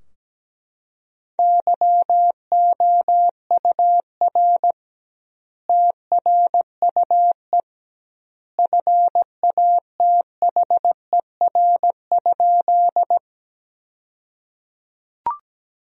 Morse code, heard as a keyed tone, about 17 wpm, 700 Hz. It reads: YOUR TRUE FATHER?
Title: 2x Word-Spacing - Sentences from Top 400 Words - 17wpm